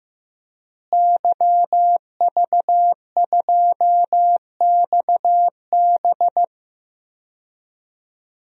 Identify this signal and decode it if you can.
Morse code, heard as a keyed tone, about 16 wpm, 700 Hz. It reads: YV2XB
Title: DXCC-100 Call Signs - 5 Characters - Repeated 2x - 15wpm